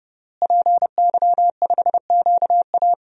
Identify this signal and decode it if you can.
Morse code, heard as a keyed tone, about 30 wpm, 700 Hz. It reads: PY5QA
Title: DXCC-100 Call Signs - 5 Characters - Repeated 3x - 30wpm